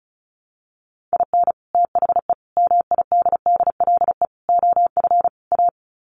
Morse code, heard as a keyed tone, about 34 wpm, 700 Hz. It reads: IN THE MIDDLE OF A
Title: Sets of 5 Words 35wpm